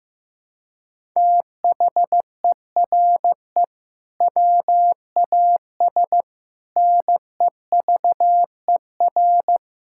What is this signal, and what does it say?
Morse code, heard as a keyed tone, about 15 wpm, 700 Hz. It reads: THERE WAS NEVER
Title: Sentences from Top 1000 Words - Review Part 6 - 15wpm